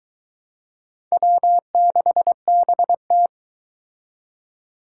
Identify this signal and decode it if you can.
Morse code, heard as a keyed tone, about 22 wpm, 700 Hz. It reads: W6BT